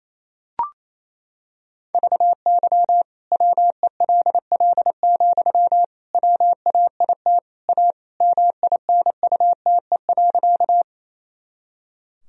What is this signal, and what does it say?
Morse code, heard as a keyed tone, about 28 wpm, 700 Hz. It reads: VY WELL, WAIT A MINUTE.
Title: Sentences from Top 500 Words and Abbreviations 28wpm